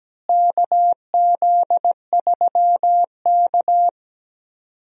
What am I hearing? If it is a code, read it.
Morse code, heard as a keyed tone, about 17 wpm, 700 Hz. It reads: KZ3K
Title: USA Call Signs - 4 Characters - Repeated 3x - 17wpm